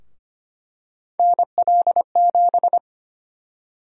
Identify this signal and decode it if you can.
Morse code, heard as a keyed tone, about 25 wpm, 700 Hz. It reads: NL7
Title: USA Call Sign Prefixes - 25wpm